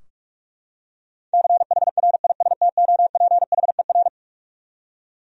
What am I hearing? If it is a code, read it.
Morse code, heard as a keyed tone, about 42 wpm, 700 Hz. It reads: CHRISTOPHER
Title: USA Common Names - 45wpm